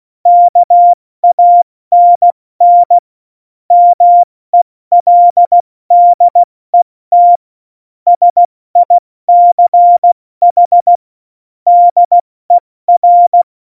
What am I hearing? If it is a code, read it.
Morse code, heard as a keyed tone, about 16 wpm, 700 Hz. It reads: KANN MELDET SICH DER
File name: Morsekurs / Witze / Witz 20. November 2018, 70 F80